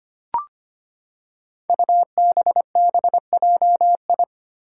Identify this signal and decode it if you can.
Morse code, heard as a keyed tone, about 25 wpm, 700 Hz. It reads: UBBJI
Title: Five Challenging Letter-Number Combinations 25wpm